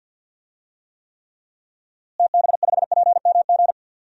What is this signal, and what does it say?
Morse code, heard as a keyed tone, about 47 wpm, 700 Hz. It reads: TB5PMG
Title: DXCC-100 Call Signs - 6 Characters - Repeated 3x - 50wpm